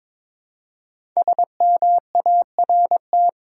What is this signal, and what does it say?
Morse code, heard as a keyed tone, about 22 wpm, 700 Hz. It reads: SMART
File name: 5 Letter Words - First 3 Letters Given - 22wpm